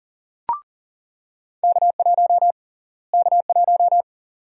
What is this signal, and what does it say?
Morse code, heard as a keyed tone, about 40 wpm, 700 Hz. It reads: K1 K1
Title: USA Call Sign Prefixes - Repeated 2x - 40wpm